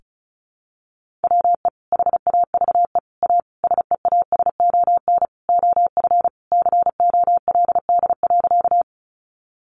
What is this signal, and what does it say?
Morse code, heard as a keyed tone, about 34 wpm, 700 Hz. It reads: WE HAVE A SEASON OF COLD.
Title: Sentences from Top 1400 Words 35wpm